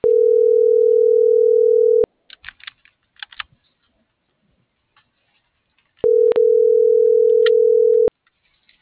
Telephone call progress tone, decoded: ringback tone

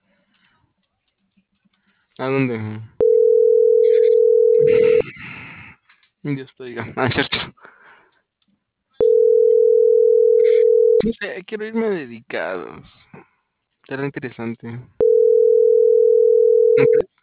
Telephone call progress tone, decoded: ringback tone

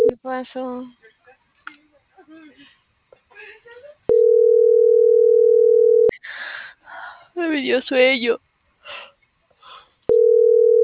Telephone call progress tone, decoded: ringback tone

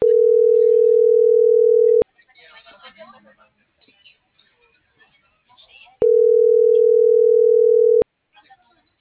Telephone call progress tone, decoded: ringback tone

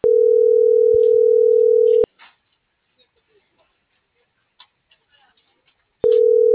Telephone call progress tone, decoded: ringback tone

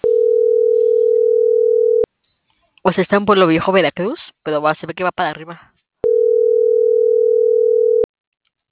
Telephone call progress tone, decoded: ringback tone

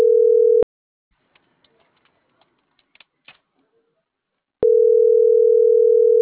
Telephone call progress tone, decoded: ringback tone

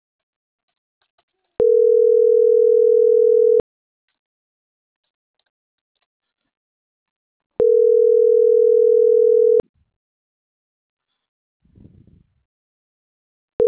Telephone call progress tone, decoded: ringback tone

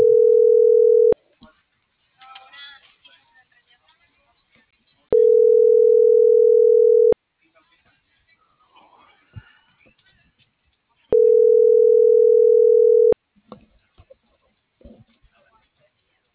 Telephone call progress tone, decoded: ringback tone

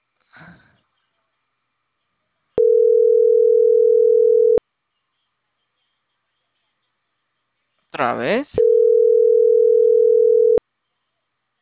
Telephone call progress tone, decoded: ringback tone

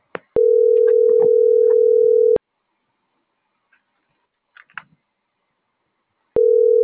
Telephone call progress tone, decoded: ringback tone

